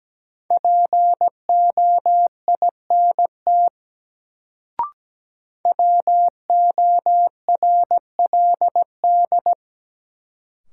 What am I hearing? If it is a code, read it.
Morse code, heard as a keyed tone, about 17 wpm, 700 Hz. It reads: POINT WORLD
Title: Sentences from Top 200 Words - New Word Drill - 17wpm